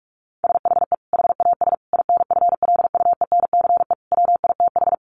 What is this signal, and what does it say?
Morse code, heard as a keyed tone, about 44 wpm, 700 Hz. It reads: SHE HAS INFLUENCE WITH
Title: Sentences from Top 1100 Words 45wpm